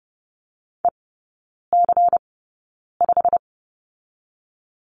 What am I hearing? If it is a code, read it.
Morse code, heard as a keyed tone, about 30 wpm, 700 Hz. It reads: E C 5
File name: Lesson 19e - Letter w - Three Characters 30wpm